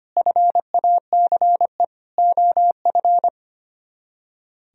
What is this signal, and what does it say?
Morse code, heard as a keyed tone, about 30 wpm, 700 Hz. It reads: FACE OF
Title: Speed-Racing - Sets of 4 Words 25wpm